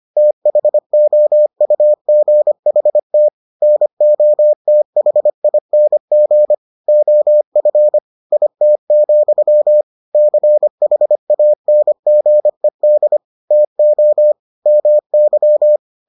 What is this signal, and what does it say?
Morse code, heard as a keyed tone, about 25 wpm, 600 Hz. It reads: THOUGHT NOTHING OF IT, CHANGED TO MY